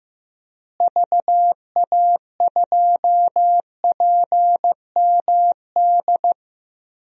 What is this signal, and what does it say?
Morse code, heard as a keyed tone, about 16 wpm, 700 Hz. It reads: VA2PMD
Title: DXCC-100 Call Signs - 6 Characters - Repeated 3x - 15wpm